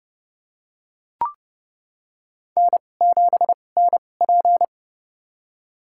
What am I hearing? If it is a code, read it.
Morse code, heard as a keyed tone, about 30 wpm, spaced as for 22 wpm, 700 Hz. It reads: N7NP
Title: USA Call Signs - 4 Characters - 2x Character Spacing - 30wpm